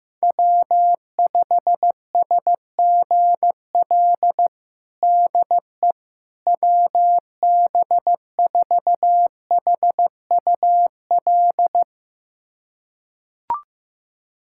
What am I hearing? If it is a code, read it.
Morse code, heard as a keyed tone, about 16 wpm, 700 Hz. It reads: W5SGL DE WB4HUL
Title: USA Callsign DE Callsign 15wpm